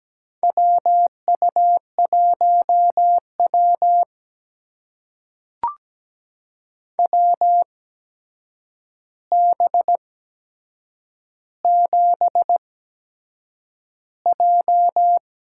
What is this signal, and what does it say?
Morse code, heard as a keyed tone, about 17 wpm, spaced as for 7 wpm, 700 Hz. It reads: WU1W WB7J
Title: USA Call Signs - 4 Characters - 8x Character Spacing - 17wpm